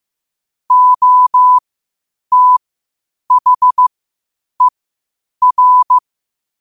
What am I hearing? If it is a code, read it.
Morse code, heard as a keyed tone, about 15 wpm, spaced as for 8 wpm, 1000 Hz. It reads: OTHER
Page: Quotes for Thu, 14 Aug 2025 in Morse Code at 8 words per minute.